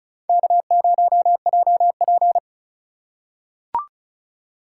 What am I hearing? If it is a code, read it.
Morse code, heard as a keyed tone, about 36 wpm, 700 Hz. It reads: K0JP
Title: USA Call Signs - 4 Characters - 4x Character Spacing - 35wpm